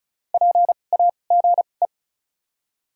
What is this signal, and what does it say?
Morse code, heard as a keyed tone, about 34 wpm, spaced as for 25 wpm, 700 Hz. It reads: PAGE
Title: 2x Character Spacing - Top 300 Words - 35wpm